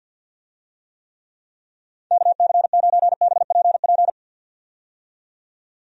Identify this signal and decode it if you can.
Morse code, heard as a keyed tone, about 49 wpm, 700 Hz. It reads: KC9BPP
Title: USA Call Signs - 6 Characters - 50wpm